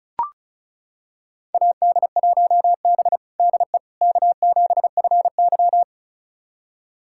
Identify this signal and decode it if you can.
Morse code, heard as a keyed tone, about 36 wpm, 700 Hz. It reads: AD1B DE K7FY
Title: USA Callsign DE Callsign - 4 Characters - 35wpm